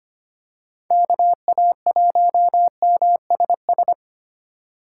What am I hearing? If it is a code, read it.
Morse code, heard as a keyed tone, about 25 wpm, 700 Hz. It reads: KA1MSS
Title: USA Call Signs - 6 Characters - 4x Character Spacing - 25wpm